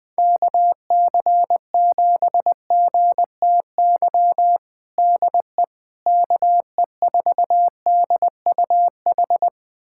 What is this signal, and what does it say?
Morse code, heard as a keyed tone, about 20 wpm, 700 Hz. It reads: KC7GTY DE KE4DUH
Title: USA Callsign DE Callsign - 6 Characters - 20wpm